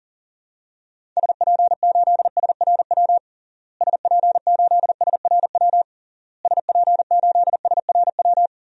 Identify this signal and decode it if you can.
Morse code, heard as a keyed tone, about 40 wpm, 700 Hz. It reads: SP8SRW SP8SRW SP8SRW
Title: DXCC-100 Call Signs - 6 Characters - Repeated 3x - 40wpm